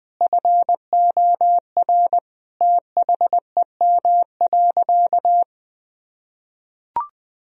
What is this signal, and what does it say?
Morse code, heard as a keyed tone, about 20 wpm, 700 Hz. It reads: FOR THEM.
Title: Sentences from Top 1000 Words - Review Part 2 - 20wpm